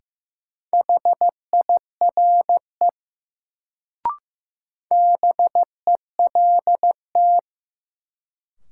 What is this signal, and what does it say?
Morse code, heard as a keyed tone, about 16 wpm, 700 Hz. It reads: HIRE BELT
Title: Sentences from Top 1400 Words - New Word Drill Alt - 15wpm